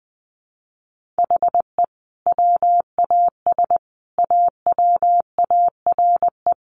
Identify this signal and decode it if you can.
Morse code, heard as a keyed tone, about 19 wpm, 700 Hz. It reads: HE WAS AWARE
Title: Sentences from Top 1200 Words 20wpm